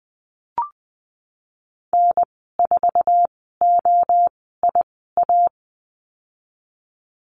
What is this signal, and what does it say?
Morse code, heard as a keyed tone, about 20 wpm, 700 Hz. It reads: N 4 O I A
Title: 076 Lesson 13g - Letter L - Call Signs 20wpm